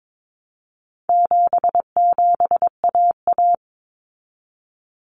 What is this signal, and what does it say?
Morse code, heard as a keyed tone, about 22 wpm, 700 Hz. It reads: 77AA
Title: Common Cut Numbers - 4-Digit 22wpm